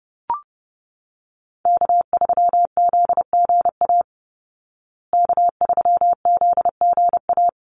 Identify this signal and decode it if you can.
Morse code, heard as a keyed tone, about 30 wpm, 700 Hz. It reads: K3ZGA K3ZGA
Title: Call Signs Repeated 3x - 30wpm